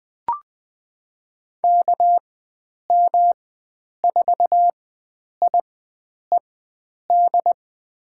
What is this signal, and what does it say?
Morse code, heard as a keyed tone, about 20 wpm, spaced as for 9 wpm, 700 Hz. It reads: KM4IED
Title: USA Call Signs - 6 Characters - 4x Character Spacing - 20wpm